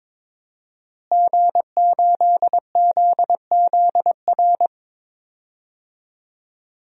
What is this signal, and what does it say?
Morse code, heard as a keyed tone, about 22 wpm, 700 Hz. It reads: G8ZZR
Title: DXCC-100 Call Signs - 5 Characters - Repeated 3x - 22wpm